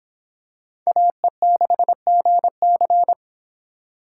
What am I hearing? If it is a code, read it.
Morse code, heard as a keyed tone, about 25 wpm, 700 Hz. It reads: AE6GC